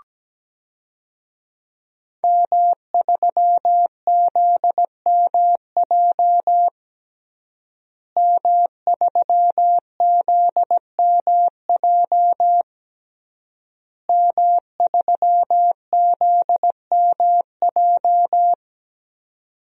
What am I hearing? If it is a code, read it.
Morse code, heard as a keyed tone, about 17 wpm, 700 Hz. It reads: M3ZMJ M3ZMJ M3ZMJ
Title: DXCC-100 Call Signs - 5 Characters - Repeated 3x - 17wpm